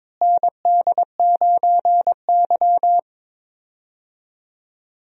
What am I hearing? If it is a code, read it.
Morse code, heard as a keyed tone, about 22 wpm, 700 Hz. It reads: ND9Y
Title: USA Call Signs - 4 Characters - Repeated 3x - 22wpm